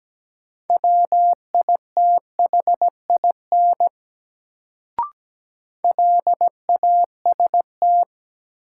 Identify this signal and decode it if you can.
Morse code, heard as a keyed tone, about 17 wpm, 700 Hz. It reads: WITHIN LAST
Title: Sets of 2 Words - New Word Drill - 17wpm